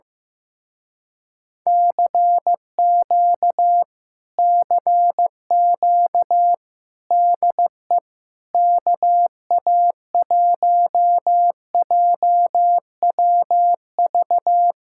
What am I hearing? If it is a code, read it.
Morse code, heard as a keyed tone, about 15 wpm, 700 Hz. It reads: CQ CQ DE KA1JWV
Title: Speed-Racing Common QSO Phrases Encore - 15wpm